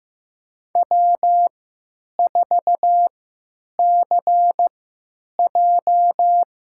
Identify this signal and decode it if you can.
Morse code, heard as a keyed tone, about 15 wpm, spaced as for 9 wpm, 700 Hz. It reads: W4CJ
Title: USA Call Signs - 4 Characters - 3x Character Spacing - 15wpm